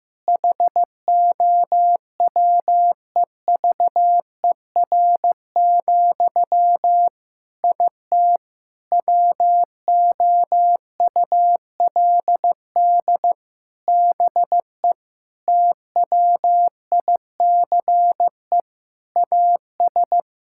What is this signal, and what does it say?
Morse code, heard as a keyed tone, about 15 wpm, 700 Hz. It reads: HOWEVER, IT WOULD BE TWICE AS
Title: Sentences from Top 1600 Words 15wpm